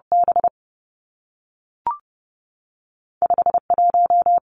Morse code, heard as a keyed tone, about 30 wpm, 700 Hz. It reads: B 5 1
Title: Two Challenging Letter-Number Combinations 30wpm